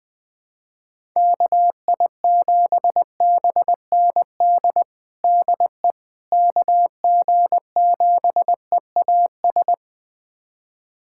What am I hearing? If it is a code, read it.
Morse code, heard as a keyed tone, about 20 wpm, 700 Hz. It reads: KI7BND DE KG7EAS
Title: USA Callsign DE Callsign - 6 Characters - 20wpm